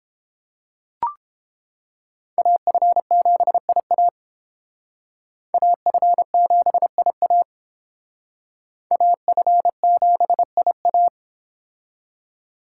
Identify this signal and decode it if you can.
Morse code, heard as a keyed tone, about 28 wpm, 690 Hz. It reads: AF7IA AF7IA AF7IA